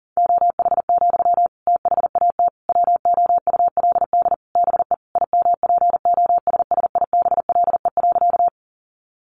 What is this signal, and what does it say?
Morse code, heard as a keyed tone, about 40 wpm, 700 Hz. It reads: OH, THAT WOULD BE IMPOSSIBLE.
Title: Sentences from Top 1100 Words 40wpm